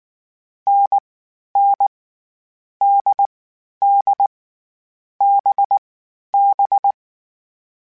Morse code, heard as a keyed tone, about 19 wpm, 800 Hz. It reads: N N D D B B